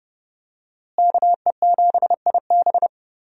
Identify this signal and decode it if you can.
Morse code, heard as a keyed tone, about 30 wpm, 700 Hz. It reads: KE7IB
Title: USA Call Signs - 5 Characters - Repeated 3x - 30wpm